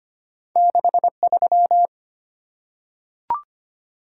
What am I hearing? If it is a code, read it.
Morse code, heard as a keyed tone, about 25 wpm, 700 Hz. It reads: 63
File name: Two Letter-Number Combinations - 25wpm